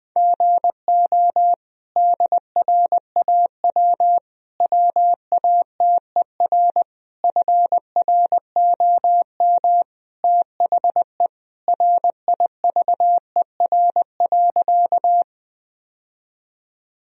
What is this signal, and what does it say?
Morse code, heard as a keyed tone, about 20 wpm, 700 Hz. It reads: GO DRAW WATER FROM THE RIVER.
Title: Sentences from Top 1000 Words - Review Part 1 - 20wpm